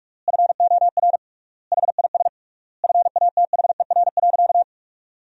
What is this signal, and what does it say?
Morse code, heard as a keyed tone, about 41 wpm, 700 Hz. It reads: FOR HIS FATHER.
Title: 3x Word-Spacing - Sentences from Top 500 Words - 45wpm